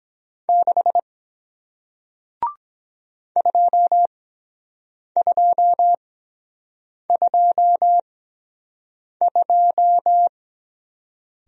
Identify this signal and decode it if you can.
Morse code, heard as a keyed tone, about 23 wpm, spaced as for 7 wpm, 700 Hz. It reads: 6 2222